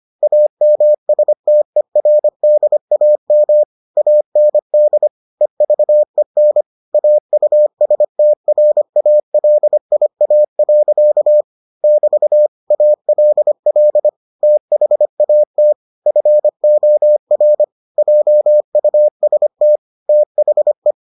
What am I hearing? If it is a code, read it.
Morse code, heard as a keyed tone, about 25 wpm, 600 Hz. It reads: AMSTERDAM AND EVEN AUSTRALIA. <BT> ALL THAT FOR JUST THE